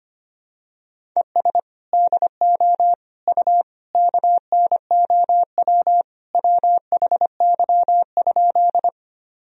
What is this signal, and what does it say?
Morse code, heard as a keyed tone, about 25 wpm, 700 Hz. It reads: ES DO U KNOW WHY?
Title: Sentences from Top 200 Words and Abbreviations 25wpm